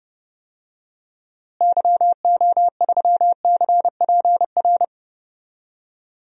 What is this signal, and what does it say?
Morse code, heard as a keyed tone, about 30 wpm, 700 Hz. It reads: YO3CPR
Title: DXCC-100 Call Signs - 6 Characters - Repeated 2x - 30wpm